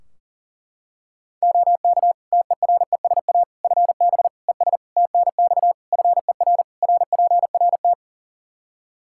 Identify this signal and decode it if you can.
Morse code, heard as a keyed tone, about 40 wpm, 700 Hz. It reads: OK TERESA FB ES TNX FER RPRT
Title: Practice QSO - Exchange 2 - Info - 40wpm